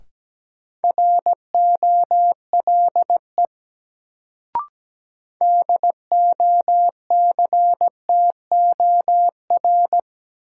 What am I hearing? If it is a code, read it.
Morse code, heard as a keyed tone, about 17 wpm, 700 Hz. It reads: ROLE DOCTOR
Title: Sentences from Top 1500 Words - New Word Drill Alt - 17wpm